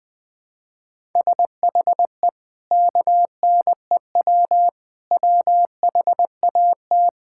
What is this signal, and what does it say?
Morse code, heard as a keyed tone, about 20 wpm, 700 Hz. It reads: SHE KNEW WHAT
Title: Sentences from Top 1000 Words - Review Part 5 - 20wpm